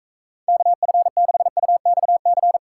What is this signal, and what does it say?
Morse code, heard as a keyed tone, about 32 wpm, 700 Hz. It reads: KF6UXC
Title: Speed Racing - USA Call Signs 28wpm